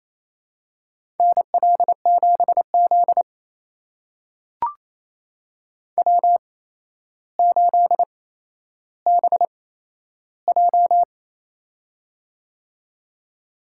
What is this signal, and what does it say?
Morse code, heard as a keyed tone, about 28 wpm, spaced as for 12 wpm, 700 Hz. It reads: NL7Z W8BJ
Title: USA Call Signs - 4 Characters - 8x Character Spacing - 28wpm